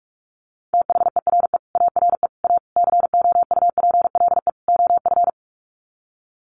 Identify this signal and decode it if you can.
Morse code, heard as a keyed tone, about 44 wpm, 700 Hz. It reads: THERE ARE A COUPLE OF
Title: Sets of 5 Words Encore 45wpm